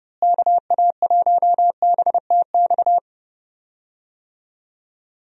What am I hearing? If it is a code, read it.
Morse code, heard as a keyed tone, about 30 wpm, 700 Hz. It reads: KA1BTX
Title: USA Call Signs - 6 Characters - Repeated 2x - 30wpm